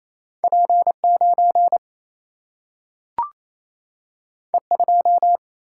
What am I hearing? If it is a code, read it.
Morse code, heard as a keyed tone, about 28 wpm, 700 Hz. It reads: P 9 E 2
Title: Single-Letter Single-Number 28wpm